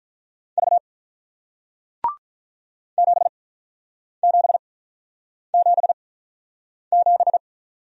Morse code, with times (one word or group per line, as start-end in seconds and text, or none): 0.6-0.8 V
3.0-7.4 7777